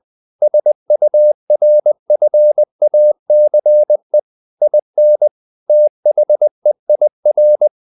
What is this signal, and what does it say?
Morse code, heard as a keyed tone, about 20 wpm, 600 Hz. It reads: SURFACE IN THEIR
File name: the old man and the sea / the old man and the sea part 1 20wpm